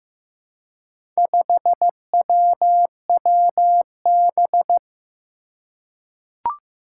Morse code, heard as a keyed tone, about 16 wpm, 700 Hz. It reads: 5WWB
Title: Four Challenging Letter-Number Combinations 15wpm